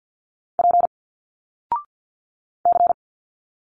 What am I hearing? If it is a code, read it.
Morse code, heard as a keyed tone, about 50 wpm, 700 Hz. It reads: P C
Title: Single Letters 50wpm